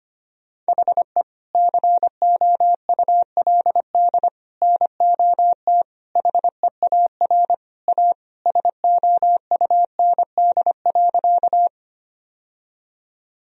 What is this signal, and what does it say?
Morse code, heard as a keyed tone, about 25 wpm, 700 Hz. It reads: HE COULD NOT HEAR A SOUND.
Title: Sentences from Top 1000 Words - Review Part 1 - 25wpm